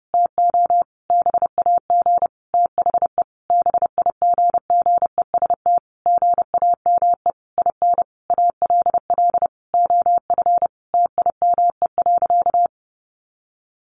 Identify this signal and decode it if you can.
Morse code, heard as a keyed tone, about 30 wpm, 700 Hz. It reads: TO BAG THE BIGGEST GAME IN ALL OF TIME.
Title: A Sound Of Thunder 30wpm